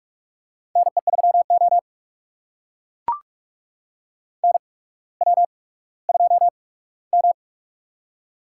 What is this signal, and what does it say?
Morse code, heard as a keyed tone, about 44 wpm, spaced as for 19 wpm, 690 Hz. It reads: NE3O NW2M
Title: USA Call Signs - 4 Characters - 8x Character Spacing - 45wpm